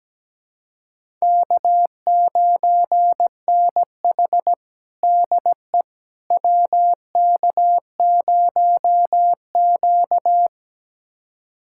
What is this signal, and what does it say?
Morse code, heard as a keyed tone, about 17 wpm, 700 Hz. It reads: K9NH DE WK0Q
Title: USA Callsign DE Callsign - 4 Characters - 17wpm